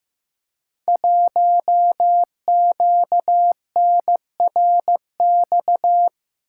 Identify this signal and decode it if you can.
Morse code, heard as a keyed tone, about 15 wpm, 700 Hz. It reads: 1QNRX
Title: Code Groups - 5 Letter:Numbers - 15wpm